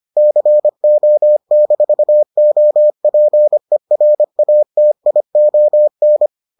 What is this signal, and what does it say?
Morse code, heard as a keyed tone, about 25 wpm, 600 Hz. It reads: CO-OPERATION